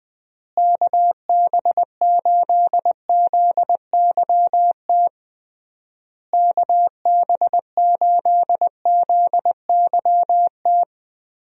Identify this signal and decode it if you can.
Morse code, heard as a keyed tone, about 20 wpm, 700 Hz. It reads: KB8ZYT KB8ZYT
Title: USA Call Signs - 6 Characters - Repeated 3x - 20wpm